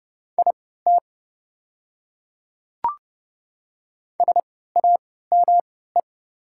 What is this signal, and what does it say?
Morse code, heard as a keyed tone, about 29 wpm, spaced as for 17 wpm, 690 Hz. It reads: IT SAME
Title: 3x Character Spacing - Top 200 Words - 30wpm